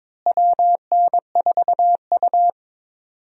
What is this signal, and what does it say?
Morse code, heard as a keyed tone, about 22 wpm, 700 Hz. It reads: WN4U
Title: USA Call Signs - 4 Characters - Repeated 2x - 22wpm